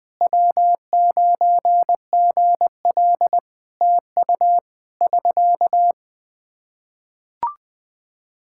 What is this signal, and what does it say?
Morse code, heard as a keyed tone, about 20 wpm, 700 Hz. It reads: W9GL TU <SK>
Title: K1USN SST QSO Practice 20wpm